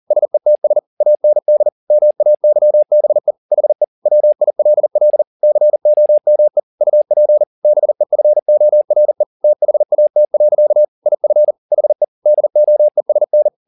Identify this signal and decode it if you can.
Morse code, heard as a keyed tone, about 40 wpm, 600 Hz. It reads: SETS AND MAYBE HE WILL COME UP BEFORE THAT. IF HE DOESN